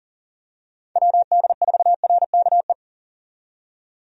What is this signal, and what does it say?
Morse code, heard as a keyed tone, about 39 wpm, 700 Hz. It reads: WD4RKE
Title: USA Call Signs - 6 Characters - 40wpm